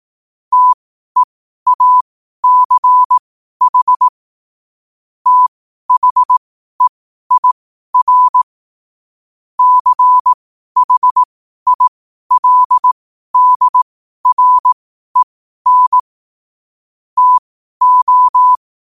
Morse code, heard as a keyed tone, about 18 wpm, spaced as for 12 wpm, 1000 Hz. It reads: TEACH THEIR CHILDREN TO